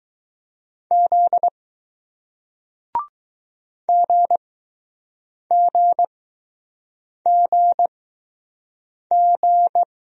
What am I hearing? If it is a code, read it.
Morse code, heard as a keyed tone, about 20 wpm, spaced as for 6 wpm, 700 Hz. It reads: Z GGGG